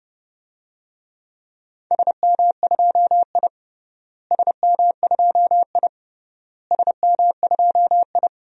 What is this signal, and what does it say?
Morse code, heard as a keyed tone, about 32 wpm, 700 Hz. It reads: SM2I SM2I SM2I
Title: DXCC-100 Call Signs - 4 Characters - Repeated 3x - 30wpm